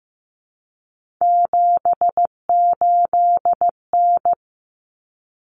Morse code, heard as a keyed tone, about 15 wpm, 700 Hz. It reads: 78N
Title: Common Cut Numbers - 3-Digit 15wpm